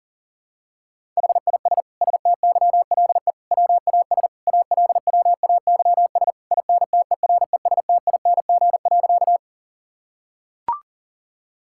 Morse code, heard as a keyed tone, about 40 wpm, 700 Hz. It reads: HIS STYLE WAS ALWAYS INTERESTING.
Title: Sentences from Top 1000 Words - Review Part 5 - 40wpm